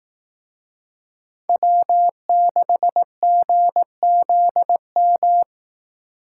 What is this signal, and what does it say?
Morse code, heard as a keyed tone, about 18 wpm, 700 Hz. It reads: W6GZM
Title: Speed Racing - USA 1x3 Call Signs 15wpm